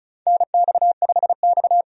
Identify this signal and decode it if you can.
Morse code, heard as a keyed tone, about 34 wpm, 700 Hz. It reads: NX5X
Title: USA Call Signs - 4 Characters - 35wpm